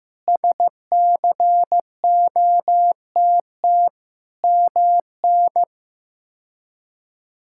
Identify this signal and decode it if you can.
Morse code, heard as a keyed tone, about 15 wpm, 700 Hz. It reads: SCOTT MN
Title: K1USN SST QSO Practice - 15wpm